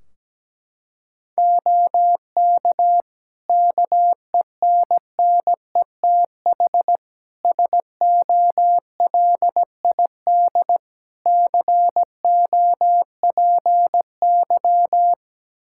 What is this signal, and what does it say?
Morse code, heard as a keyed tone, about 17 wpm, 700 Hz. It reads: OK KENNETH SOLID COPY
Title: Practice QSO - Exchange 3 - More Info - 17wpm